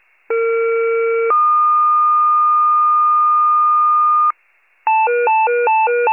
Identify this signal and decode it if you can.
Two-tone page, heard as a one-second tone, then a three-second tone, about 460 then 1220 Hz. It is heard from 0.3 to 4.3 s.